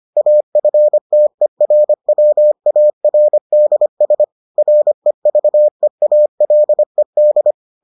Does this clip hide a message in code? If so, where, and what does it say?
Yes: Morse code from 0.2 to 7.5 s, heard as a keyed tone, about 25 wpm, 600 Hz. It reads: AFTERWARDS REVEALED